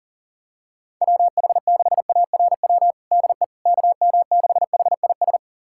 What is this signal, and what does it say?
Morse code, heard as a keyed tone, about 40 wpm, 700 Hz. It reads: WH6ARW DE KM6HIS
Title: USA Callsign DE Callsign - 6 Characters - 40wpm